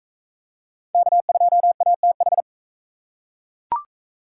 Morse code, heard as a keyed tone, about 38 wpm, 700 Hz. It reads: K2ATH